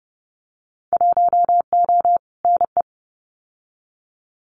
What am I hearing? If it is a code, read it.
Morse code, heard as a keyed tone, about 30 wpm, 700 Hz. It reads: 1O NE
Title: Winter Field Day Exchanges Encore 30wpm